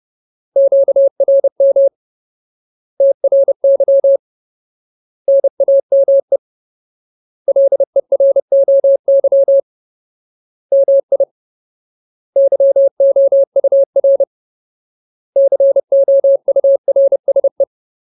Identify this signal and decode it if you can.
Morse code, heard as a keyed tone, about 30 wpm, 550 Hz. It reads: QRM TRY NAME LEROY MI YOUR COURSE
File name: Ultimate Word List / 30WPM Ultimate